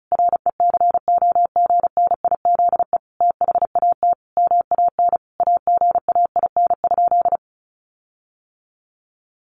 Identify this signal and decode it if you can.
Morse code, heard as a keyed tone, about 34 wpm, 700 Hz. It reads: RECOGNIZE THAT MAN AGAIN?
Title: Sentences from Top 1400 Words 35wpm